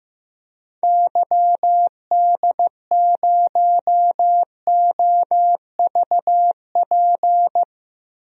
Morse code, heard as a keyed tone, about 15 wpm, 700 Hz. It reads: YD0OVP
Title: DXCC-100 Call Signs - 6 Characters - Repeated 3x - 15wpm